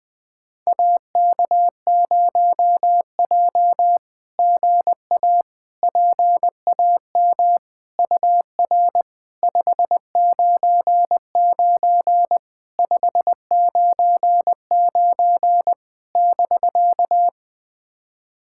Morse code, heard as a keyed tone, about 20 wpm, 690 Hz. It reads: AK0J GA PAM UR 599 599 <BK>
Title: SOTA QSO Practice 20wpm